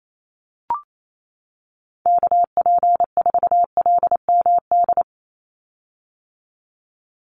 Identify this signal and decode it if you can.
Morse code, heard as a keyed tone, about 28 wpm, 700 Hz. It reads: KP4LMD
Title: USA Call Signs 28wpm